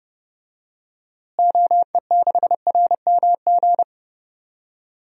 Morse code, heard as a keyed tone, about 30 wpm, 700 Hz. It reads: OE6RMG
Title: DXCC-100 Call Signs - 6 Characters - Repeated 2x - 30wpm